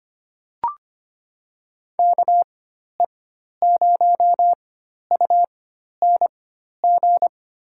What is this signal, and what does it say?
Morse code, heard as a keyed tone, about 25 wpm, spaced as for 12 wpm, 700 Hz. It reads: KE0UNG
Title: USA Call Signs - 6 Characters - 4x Character Spacing - 25wpm